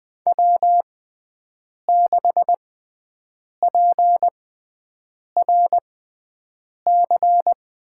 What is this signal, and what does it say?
Morse code, heard as a keyed tone, about 20 wpm, spaced as for 7 wpm, 700 Hz. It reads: W6PRC
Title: USA Call Signs - 6x Character Spacing - 20wpm